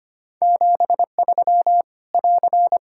Morse code, heard as a keyed tone, about 25 wpm, 700 Hz. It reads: 73 <AR>
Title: Practice QSO - Exchange 4 - Ending - 25wpm